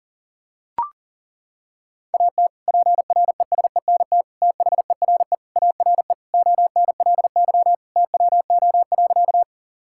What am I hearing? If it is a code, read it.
Morse code, heard as a keyed tone, about 40 wpm, 700 Hz. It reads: AT PRESENT THERE ARE ONLY TWO.
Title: Sentences from Top 1000 Words - Review Part 5 - 40wpm